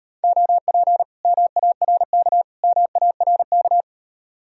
Morse code, heard as a keyed tone, about 33 wpm, 700 Hz. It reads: OP MARK MARK